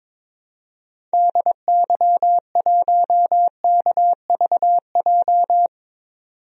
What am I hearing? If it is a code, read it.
Morse code, heard as a keyed tone, about 22 wpm, 700 Hz. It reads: DY1KVJ
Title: DXCC-100 Call Signs - 6 Characters - 22wpm